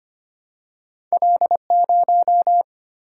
Morse code, heard as a keyed tone, about 25 wpm, 700 Hz. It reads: L0
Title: Single-Letter Single-Number 25wpm